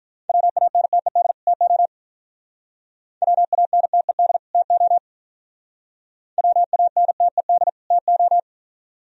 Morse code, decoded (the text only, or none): WANTED TO WANTED TO WANTED TO